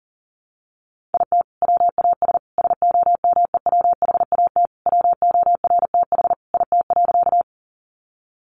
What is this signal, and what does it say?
Morse code, heard as a keyed tone, about 40 wpm, 700 Hz. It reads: IT WAS SOMEWHAT WORTH IT.
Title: Sentences from Top 1300 Words 40wpm